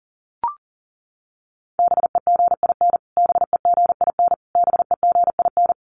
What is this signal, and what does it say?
Morse code, heard as a keyed tone, about 40 wpm, 700 Hz. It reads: BEGIN BEGIN BEGIN
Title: Top 300 Words Repeated 40wpm